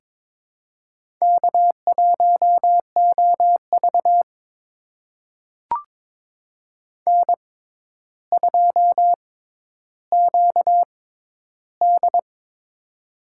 Morse code, heard as a keyed tone, about 22 wpm, spaced as for 11 wpm, 700 Hz. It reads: K1OV N2QD
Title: USA Call Signs - 4 Characters - 6x Character Spacing - 22wpm